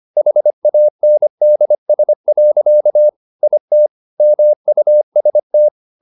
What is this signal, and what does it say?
Morse code, heard as a keyed tone, about 25 wpm, 600 Hz. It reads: HANDS. IT MUST
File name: the old man and the sea part 1 25wpm